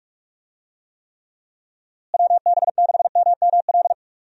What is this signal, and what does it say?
Morse code, heard as a keyed tone, about 41 wpm, 700 Hz. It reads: WB6MML
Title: USA Call Signs - 6 Characters - Repeated 3x - 45wpm